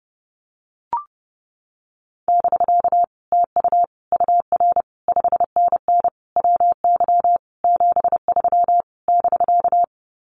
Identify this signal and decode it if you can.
Morse code, heard as a keyed tone, about 30 wpm, 700 Hz. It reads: <BK> TU UR 5NN WY 73 <BK>
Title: Common QSO Phrases Encore 30wpm